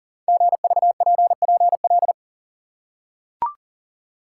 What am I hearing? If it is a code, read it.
Morse code, heard as a keyed tone, about 42 wpm, 700 Hz. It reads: GVPPL
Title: Code Groups - 5 Letters - 40wpm